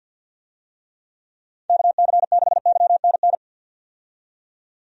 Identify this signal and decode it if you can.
Morse code, heard as a keyed tone, about 52 wpm, 700 Hz. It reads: KC6YNN
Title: USA Call Signs - 6 Characters - 50wpm